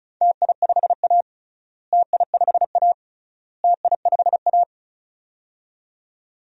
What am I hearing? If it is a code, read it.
Morse code, heard as a keyed tone, about 36 wpm, 700 Hz. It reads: TI5A TI5A TI5A
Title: DXCC-100 Call Signs - 4 Characters - Repeated 3x - 35wpm